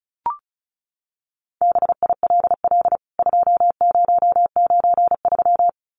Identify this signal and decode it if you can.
Morse code, heard as a keyed tone, about 34 wpm, 700 Hz. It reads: BILL 2093
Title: CWT Exchange 35wpm